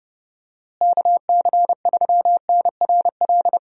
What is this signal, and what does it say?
Morse code, heard as a keyed tone, about 30 wpm, 700 Hz. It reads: KC3NRL
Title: USA Call Signs - 6 Characters - Repeated 3x - 30wpm